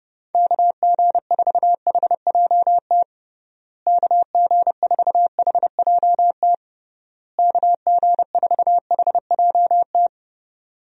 Morse code, heard as a keyed tone, about 30 wpm, 700 Hz. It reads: KG4HJT KG4HJT KG4HJT
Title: USA Call Signs - 6 Characters - Repeated 3x - 30wpm